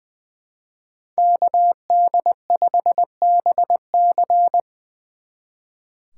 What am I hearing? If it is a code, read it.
Morse code, heard as a keyed tone, about 23 wpm, 700 Hz. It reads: KD5BC
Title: Speed Racing - USA 2x2 Call Signs 20wpm